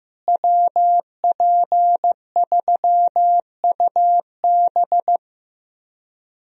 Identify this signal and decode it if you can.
Morse code, heard as a keyed tone, about 15 wpm, 700 Hz. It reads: WP3UB
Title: USA Call Signs - 5 Characters - Repeated 2x - 15wpm